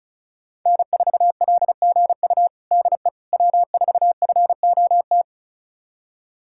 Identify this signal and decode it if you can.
Morse code, heard as a keyed tone, about 34 wpm, 700 Hz. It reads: N4LGU DE W4FOT
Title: USA Callsign DE Callsign - 5 Characters - 35wpm